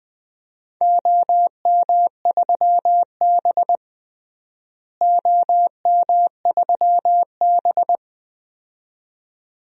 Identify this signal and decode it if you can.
Morse code, heard as a keyed tone, about 20 wpm, 700 Hz. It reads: OM3B OM3B
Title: DXCC-100 Call Signs - 4 Characters - Repeated 2x - 20wpm